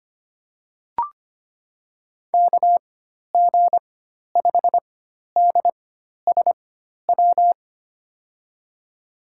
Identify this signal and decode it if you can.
Morse code, heard as a keyed tone, about 25 wpm, spaced as for 12 wpm, 700 Hz. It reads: KG5DSW
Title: USA Call Signs - 4x Character Spacing - 25wpm